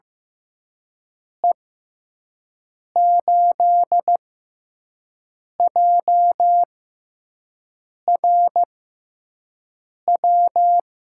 Morse code, heard as a keyed tone, about 15 wpm, spaced as for 5 wpm, 700 Hz. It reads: E8JRW
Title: USA Call Signs - 6 Characters - 6x Character Spacing - 15wpm